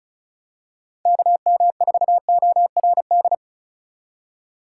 Morse code, heard as a keyed tone, about 36 wpm, 700 Hz. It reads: KM4ORD
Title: USA Call Signs - 3x Character Spacing - 35wpm